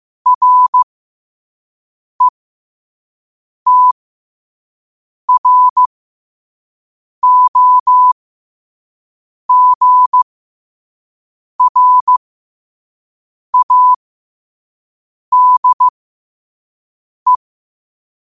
Morse code, heard as a keyed tone, about 15 wpm, spaced as for 5 wpm, 1000 Hz. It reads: RETROGRADE
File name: qotdfiles / 5wpm